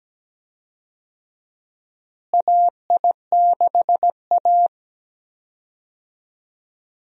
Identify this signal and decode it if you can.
Morse code, heard as a keyed tone, about 17 wpm, 700 Hz. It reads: AI6A
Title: USA Call Signs - 4 Characters - Repeated 2x - 17wpm